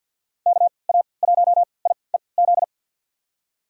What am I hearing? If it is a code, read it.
Morse code, heard as a keyed tone, about 46 wpm, spaced as for 27 wpm, 700 Hz. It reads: KA1IEZ
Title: USA Call Signs - 3x Character Spacing - 50wpm